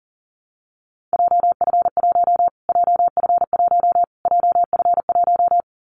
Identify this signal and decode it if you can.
Morse code, heard as a keyed tone, about 40 wpm, 700 Hz. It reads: JF1 JF1 JF1
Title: Callsign Prefixes for Top 25 DXCC - Repeated 3x - 40wpm